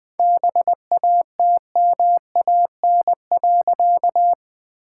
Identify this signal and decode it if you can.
Morse code, heard as a keyed tone, about 20 wpm, 700 Hz. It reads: BATMAN.
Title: All-Age Appropriate Jokes - 20wpm